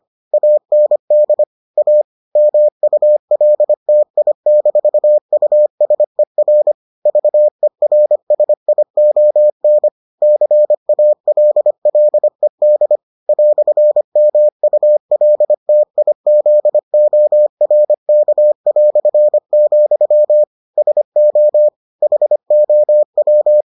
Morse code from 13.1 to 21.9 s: "MULTIZORK", SO

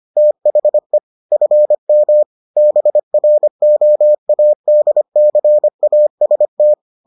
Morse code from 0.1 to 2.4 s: THE FM

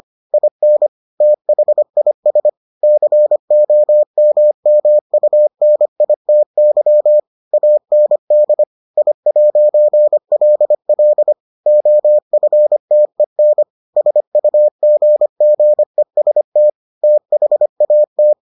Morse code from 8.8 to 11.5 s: I'LL

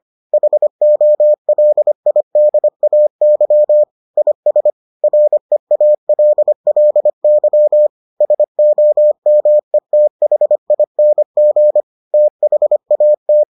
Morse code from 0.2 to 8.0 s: HOLIDAY IS REALLY

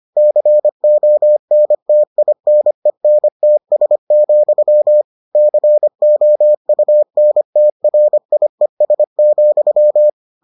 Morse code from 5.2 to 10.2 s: COUNTRIES,

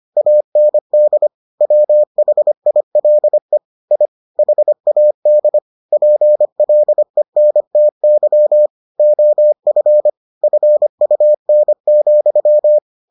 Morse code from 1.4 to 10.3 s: WHILE I HAD PLENTY OF

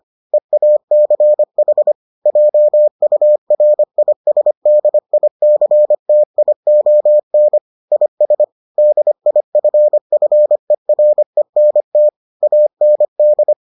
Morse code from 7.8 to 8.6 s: IS

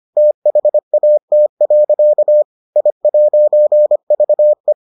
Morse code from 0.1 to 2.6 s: THAT.

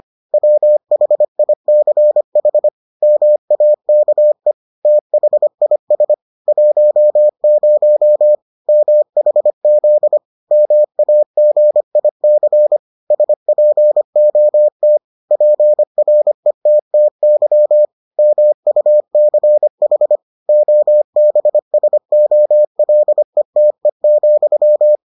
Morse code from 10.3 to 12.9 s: MAGIC

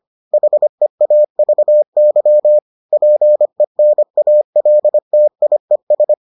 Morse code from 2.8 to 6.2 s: PENALTIES

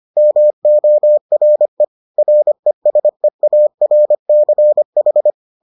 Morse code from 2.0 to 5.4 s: RESEARCH

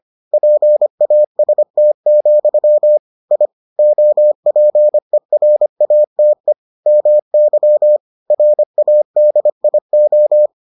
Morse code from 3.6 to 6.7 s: OPERATE

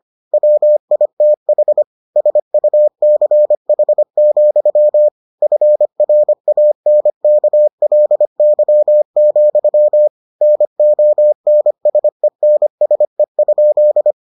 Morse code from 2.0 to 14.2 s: SUCH, FRANKLY, NONSENSE?

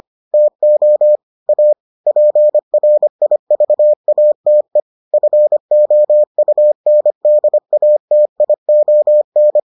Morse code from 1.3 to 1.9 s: A

